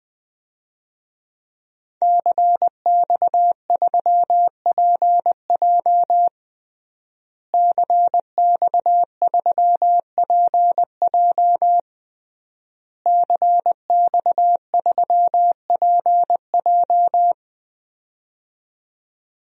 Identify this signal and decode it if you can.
Morse code, heard as a keyed tone, about 21 wpm, 700 Hz. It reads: CX3PJ CX3PJ CX3PJ
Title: DXCC-100 Call Signs - 5 Characters - Repeated 3x - 20wpm